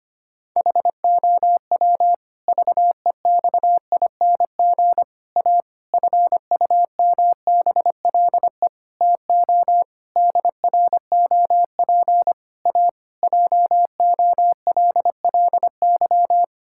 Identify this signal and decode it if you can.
Morse code, heard as a keyed tone, about 25 wpm, 700 Hz. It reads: HOW VEXING A FUMBLE TO DROP A JOLLY
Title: Pangram Challenge v2 - 25wpm